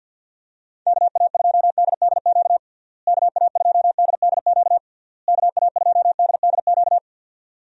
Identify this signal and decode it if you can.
Morse code, heard as a keyed tone, about 50 wpm, 700 Hz. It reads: KA2DDQ KA2DDQ KA2DDQ
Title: USA Call Signs - 6 Characters - Repeated 3x - 50wpm